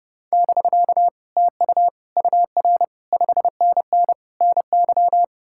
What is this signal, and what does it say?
Morse code, heard as a keyed tone, about 38 wpm, 700 Hz. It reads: <BK> TU UR 5NN NY